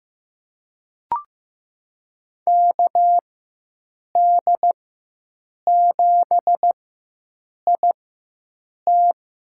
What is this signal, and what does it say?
Morse code, heard as a keyed tone, about 15 wpm, spaced as for 7 wpm, 700 Hz. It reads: KD7IT
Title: USA Call Signs - 5 Characters - 4x Character Spacing - 15wpm